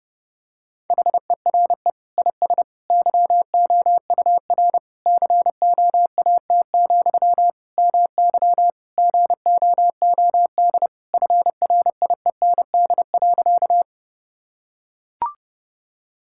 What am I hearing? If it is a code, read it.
Morse code, heard as a keyed tone, about 32 wpm, 700 Hz. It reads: HERE IS YOUR COAT, MY GOOD FRIEND.
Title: Sentences from Top 1600 Words 30wpm